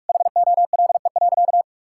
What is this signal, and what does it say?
Morse code, heard as a keyed tone, about 46 wpm, 700 Hz. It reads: HOLE.
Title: Sentences from Top 1600 Words 45wpm